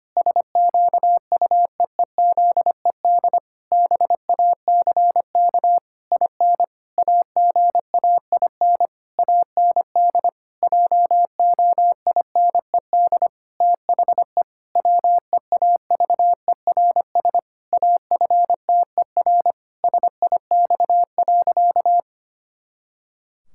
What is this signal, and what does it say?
Morse code, heard as a keyed tone, about 25 wpm, 700 Hz. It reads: SQUEEZED BACK IN AGAIN AND JOINED THE WEAVERS AFTER SIX.
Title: Pangram Challenge v2 - 25wpm